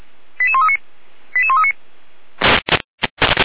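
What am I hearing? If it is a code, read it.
Five-tone selective call (ZVEI), heard as quick runs of five tones, tones of about 70 ms. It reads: 88139 88139
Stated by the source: ZVEI five-tone selective call received via SDR, from the feuerwehr-orlamuende.org server (Feuerwehr Orlamünde, fire brigade).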